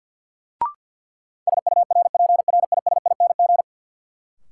Morse code, heard as a keyed tone, about 52 wpm, 700 Hz. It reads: SURPRISING
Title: Sets of 5 Words - New Word Drill - 50wpm